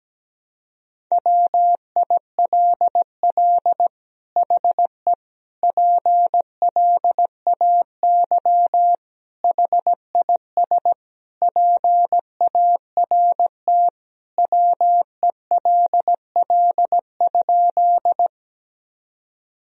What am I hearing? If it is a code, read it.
Morse code, heard as a keyed tone, about 17 wpm, 700 Hz. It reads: WILL HE PLAY HIS PART WELL?
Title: Sentences from Top 1000 Words - Review Part 1 - 17wpm